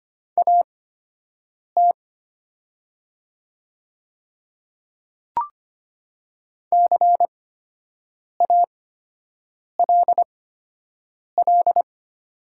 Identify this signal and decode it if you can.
Morse code, heard as a keyed tone, about 25 wpm, spaced as for 7 wpm, 700 Hz. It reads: AT CALL
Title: 8x Character Spacing - Top 100 Words - 25wpm